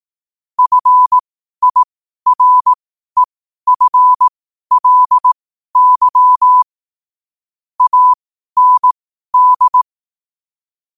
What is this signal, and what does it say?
Morse code, heard as a keyed tone, about 18 wpm, spaced as for 12 wpm, 1000 Hz. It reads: FIREFLY AND
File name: qotdfiles / 12wpm